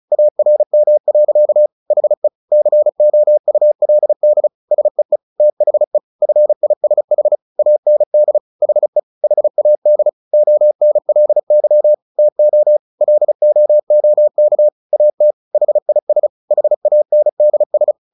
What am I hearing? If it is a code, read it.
Morse code, heard as a keyed tone, about 34 wpm, 600 Hz. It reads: ARM. HE COULD SEE THE FISH AND HE HAD ONLY TO LOOK AT HIS HANDS
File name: the old man and the sea 2 35wpm